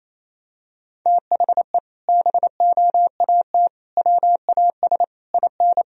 Morse code, heard as a keyed tone, about 28 wpm, 700 Hz. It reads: THE BOAT WAS IN
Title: Sentences from Top 1000 Words - Review Part 1 - 28wpm